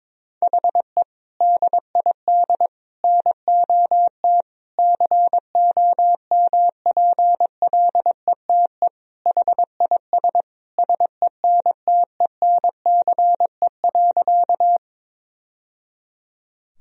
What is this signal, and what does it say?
Morse code, heard as a keyed tone, about 22 wpm, 700 Hz. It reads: HE DID NOT COMPLETE HIS SENTENCE.
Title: Sentences from Top 1000 Words - Review Part 1 - 22wpm